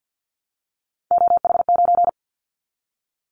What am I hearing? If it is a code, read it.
Morse code, heard as a keyed tone, about 50 wpm, 700 Hz. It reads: OH9
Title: Callsign Prefixes for Top 50 DXCC - 50wpm